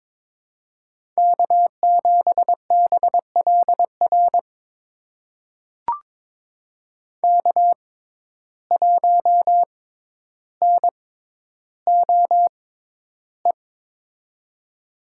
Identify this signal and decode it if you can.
Morse code, heard as a keyed tone, about 22 wpm, spaced as for 11 wpm, 700 Hz. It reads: K7BLR K1NOE
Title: USA Call Signs - 5 Characters - 6x Character Spacing - 22wpm